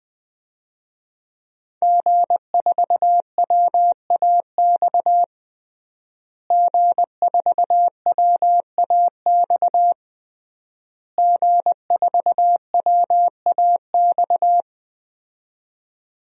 Morse code, heard as a keyed tone, about 21 wpm, 700 Hz. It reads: G4WAX G4WAX G4WAX
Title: DXCC-100 Call Signs - 5 Characters - Repeated 3x - 20wpm